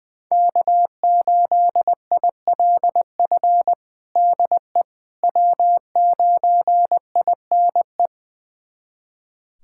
Morse code, heard as a keyed tone, about 20 wpm, 700 Hz. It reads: K8ILF DE W9INE
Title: USA Callsign DE Callsign - 5 Characters - 20wpm